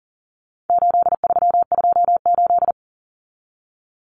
Morse code, heard as a keyed tone, about 37 wpm, 700 Hz. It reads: 8328
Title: Numbers - 4-Digit 40wpm